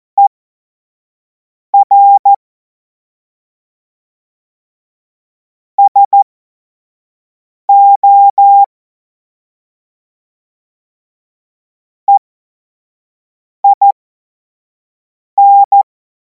Morse code, heard as a keyed tone, about 14 wpm, spaced as for 5 wpm, 800 Hz. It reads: ER SO EIN
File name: Morsekurs / Witze / Witz 16. Juni 2026 25 F70